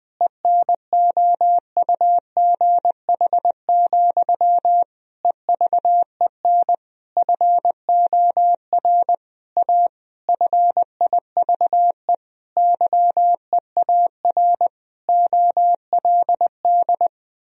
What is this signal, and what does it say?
Morse code, heard as a keyed tone, about 20 wpm, 700 Hz. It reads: ENOUGH, EVEN FOR A FIVE YEAR OLD
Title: Sentences from Top 1000 Words - Review Part 2 - 20wpm